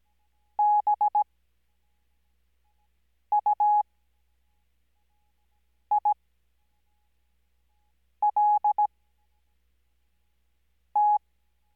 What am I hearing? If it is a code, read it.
Morse code, heard as a keyed tone, about 17 wpm, spaced as for 4 wpm, 840 Hz. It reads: BUILT
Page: Farnsworth Spacing: 15WPM Character speed Spaced at 5WPM.